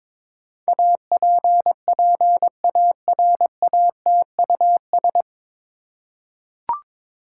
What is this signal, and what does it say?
Morse code, heard as a keyed tone, about 22 wpm, 700 Hz. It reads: APPARATUS
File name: 9 Letter Words - First 3 Letters Given - 22wpm